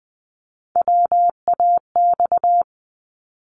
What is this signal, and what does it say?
Morse code, heard as a keyed tone, about 20 wpm, 700 Hz. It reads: WAX
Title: Multi-letter Phonemes Repeated 3x 20wpm